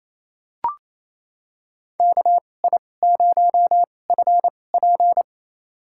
Morse code, heard as a keyed tone, about 28 wpm, spaced as for 20 wpm, 700 Hz. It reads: KI0FP
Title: USA Call Signs - 5 Characters - 2x Character Spacing - 28wpm